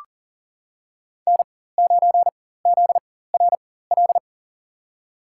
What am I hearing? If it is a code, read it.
Morse code, heard as a keyed tone, about 40 wpm, spaced as for 19 wpm, 700 Hz. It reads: N9ZRL
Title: USA Call Signs - 5 Characters - 4x Character Spacing - 40wpm